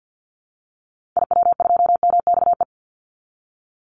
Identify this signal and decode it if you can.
Morse code, heard as a keyed tone, about 50 wpm, 700 Hz. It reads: IW2MXE
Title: Call Signs Repeated 2x - 50wpm